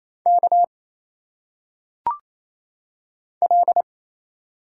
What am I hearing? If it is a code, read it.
Morse code, heard as a keyed tone, about 27 wpm, 700 Hz. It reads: K L
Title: Single Letters - Phonetic Pronunciation 28wpm